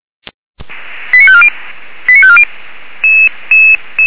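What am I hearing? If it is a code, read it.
Five-tone selective call (ZVEI), heard as quick runs of five tones, tones of about 70 ms. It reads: 89540 89540
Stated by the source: ZVEI five-tone call 89999 received by SDR.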